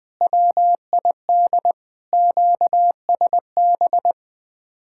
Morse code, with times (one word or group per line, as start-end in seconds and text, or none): 0.2-1.7 WID
2.1-4.1 QSB